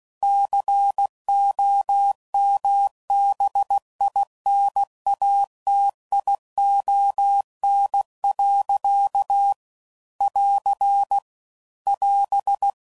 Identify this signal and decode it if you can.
Morse code, heard as a keyed tone, about 16 wpm, 800 Hz. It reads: COMBINATION. <AR> <AS>
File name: CW Code Groups @15WPM/Technical Subjects